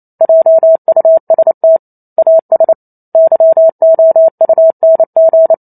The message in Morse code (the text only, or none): JUST AS YOUNG